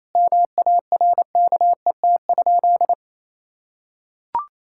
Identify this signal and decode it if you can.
Morse code, heard as a keyed tone, about 28 wpm, 700 Hz. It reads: MARKET?
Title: Sentences from Top 1000 Words - Review Part 6 - 28wpm